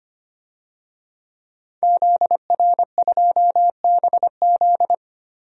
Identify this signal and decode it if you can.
Morse code, heard as a keyed tone, about 25 wpm, 700 Hz. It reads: ZR2BZ
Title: DXCC-100 Call Signs - 5 Characters - 25wpm